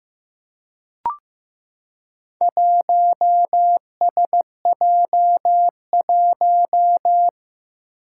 Morse code, heard as a keyed tone, about 15 wpm, 700 Hz. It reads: 1SJ1
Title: Four Challenging Letter-Number Combinations 15wpm